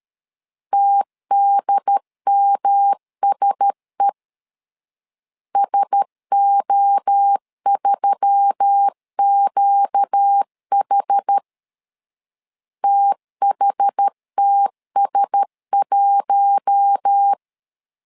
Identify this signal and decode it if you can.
Morse code, heard as a keyed tone, about 13 wpm, 800 Hz. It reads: TDMSE SO3QH THTS1
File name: LEZIONI / 4 Q1S3 12wpm 2023